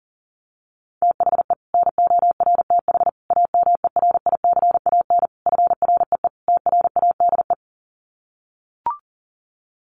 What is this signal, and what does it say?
Morse code, heard as a keyed tone, about 40 wpm, 700 Hz. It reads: THE NORTH AMERICAN FREE TRADE
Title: Sets of 5 Words Encore 40wpm